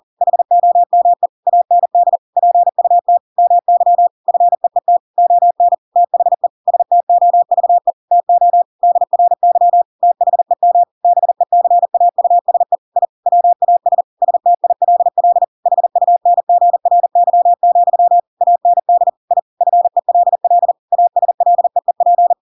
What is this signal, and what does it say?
Morse code, heard as a keyed tone, about 40 wpm, 700 Hz. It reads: HOME AND PUT MY FEET ON THE STOVE TO DRY THEM BECAUSE I WAS STILL HUNGRY, AND I FELL ASLEEP
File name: Pinocchio / Pinocchio ch7 40 40 ik1hge